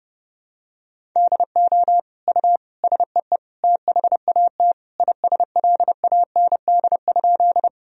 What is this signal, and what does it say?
Morse code, heard as a keyed tone, about 32 wpm, 700 Hz. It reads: DO U SEE THAT ISLAND?
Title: Sentences from Top 500 Words and Abbreviations 30wpm